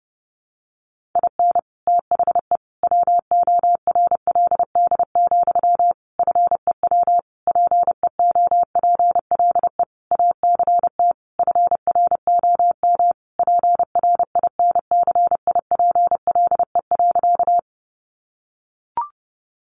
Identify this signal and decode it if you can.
Morse code, heard as a keyed tone, about 30 wpm, 700 Hz. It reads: IN THE WORLD, FEW PEOPLE ACT FROM PRINCIPLE.
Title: Sentences from Top 1400 Words 30wpm